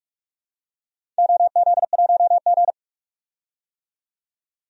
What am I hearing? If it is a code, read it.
Morse code, heard as a keyed tone, about 46 wpm, 700 Hz. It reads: OZ1G
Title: DXCC-100 Call Signs - 4 Characters - 45wpm